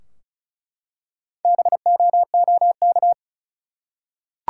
Morse code, heard as a keyed tone, about 36 wpm, 700 Hz. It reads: BOOK
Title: Four Letter Words - 35wpm